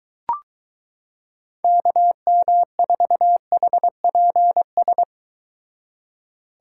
Morse code, heard as a keyed tone, about 23 wpm, 700 Hz. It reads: KM4HPS